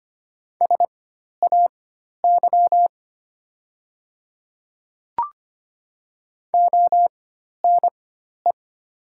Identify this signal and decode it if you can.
Morse code, heard as a keyed tone, about 25 wpm, spaced as for 12 wpm, 690 Hz. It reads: SAY ONE
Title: 4x Character Spacing - Top 300 Words - 25wpm